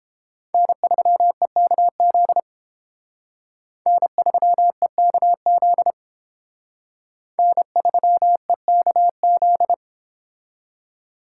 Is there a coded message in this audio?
Yes: Morse code from 0.5 to 9.7 s, heard as a keyed tone, about 30 wpm, 700 Hz. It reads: N3EKZ N3EKZ N3EKZ